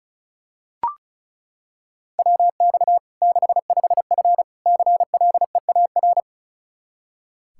Morse code, heard as a keyed tone, about 34 wpm, 700 Hz. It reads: WX 65F CLEAR
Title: Practice QSO - Exchange 2 - Info - 35wpm